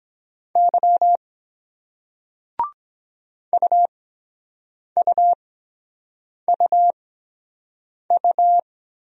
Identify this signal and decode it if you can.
Morse code, heard as a keyed tone, about 23 wpm, spaced as for 7 wpm, 700 Hz. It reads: Y UUUU